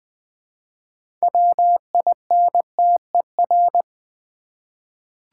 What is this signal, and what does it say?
Morse code, heard as a keyed tone, about 20 wpm, 700 Hz. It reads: WINTER
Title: Six Letter Words - 20wpm